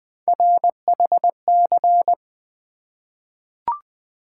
Morse code, heard as a keyed tone, about 20 wpm, 700 Hz. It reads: RHC
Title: Three Letter-Number Combinations - 20wpm